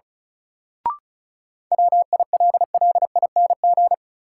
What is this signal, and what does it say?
Morse code, heard as a keyed tone, about 36 wpm, 700 Hz. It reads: WILLING
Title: Sentences from Top 1000 Words - New Word Drill - 35wpm